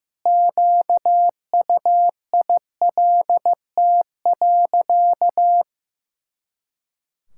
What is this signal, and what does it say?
Morse code, heard as a keyed tone, about 15 wpm, 700 Hz. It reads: QUILT.
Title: Pangram Challenge v2 - 15wpm